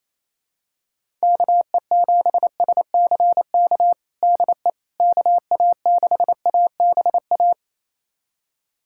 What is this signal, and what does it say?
Morse code, heard as a keyed tone, about 27 wpm, 700 Hz. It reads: KE7SCK DE KA6ABA
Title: USA Callsign DE Callsign - 6 Characters - 28wpm